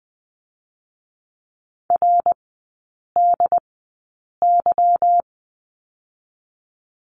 Morse code, heard as a keyed tone, about 20 wpm, spaced as for 8 wpm, 690 Hz. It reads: RDY
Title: 198 Lesson 32e - Letter k - Three Characters 20wpm